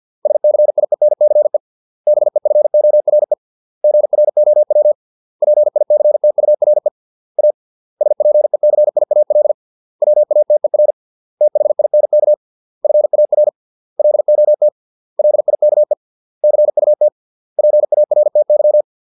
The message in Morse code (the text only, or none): SCIENCE BEFORE GROW PICTURE A SPECIAL WATER THINK FAR LOT LIKE CUT PARTY